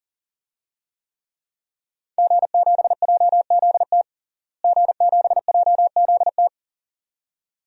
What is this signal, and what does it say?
Morse code, heard as a keyed tone, about 40 wpm, 700 Hz. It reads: G7JZT G7JZT
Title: DXCC-100 Call Signs - 5 Characters - Repeated 2x - 40wpm